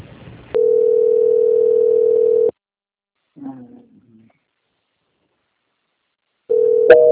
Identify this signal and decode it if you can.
Ringback tone (North America), 440 + 480 Hz.
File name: out- -2058- - - .168